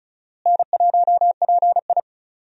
Morse code, heard as a keyed tone, about 36 wpm, 700 Hz. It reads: N1PI
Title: USA Call Signs - 4 Characters - Repeated 2x - 35wpm